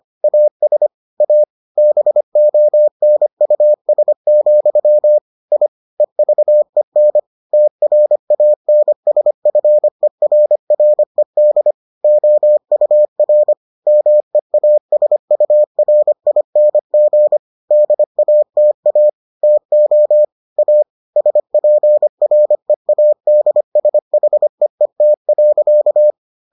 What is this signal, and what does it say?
Morse code, heard as a keyed tone, about 25 wpm, 600 Hz. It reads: AS A BONUS, I EVEN TRANSFERRED OUR MEASURING DATA TO A SPREADSHEET.